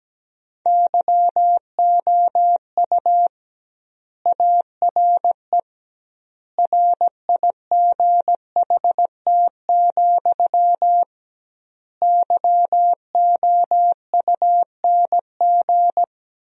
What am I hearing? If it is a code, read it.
Morse code, heard as a keyed tone, about 17 wpm, 700 Hz. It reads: YOU ARE RIGHT, YOUNG
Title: 2x Word-Spacing - Sentences from Top 400 Words - 17wpm